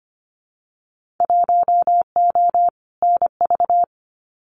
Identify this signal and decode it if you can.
Morse code, heard as a keyed tone, about 25 wpm, 700 Hz. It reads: 1O NV
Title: Winter Field Day Exchanges Encore 25wpm